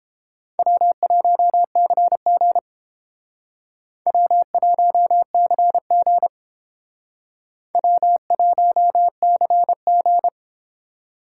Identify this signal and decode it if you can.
Morse code, heard as a keyed tone, about 30 wpm, 700 Hz. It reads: W1CG W1CG W1CG